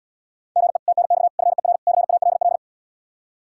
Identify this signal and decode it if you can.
Morse code, heard as a keyed tone, about 80 wpm, 700 Hz. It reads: GE TNX FER CALL